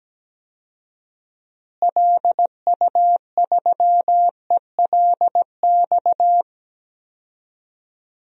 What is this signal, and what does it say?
Morse code, heard as a keyed tone, about 17 wpm, 700 Hz. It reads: LU3ELX
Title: DXCC-100 Call Signs - 6 Characters - Repeated 3x - 17wpm